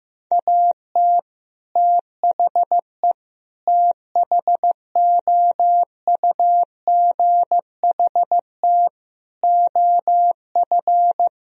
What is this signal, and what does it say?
Morse code, heard as a keyed tone, about 16 wpm, 700 Hz. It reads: AT THE THOUGHT OF
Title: Sentences from Top 1000 Words - Review Part 2 - 15wpm